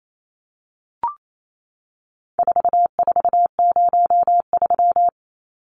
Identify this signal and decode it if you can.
Morse code, heard as a keyed tone, about 28 wpm, 700 Hz. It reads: 4403
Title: Numbers - 4-Digit 28wpm